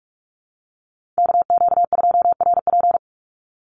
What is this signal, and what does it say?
Morse code, heard as a keyed tone, about 46 wpm, 700 Hz. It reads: KQ2RP
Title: NA SOTA Chaser Callsigns 45wpm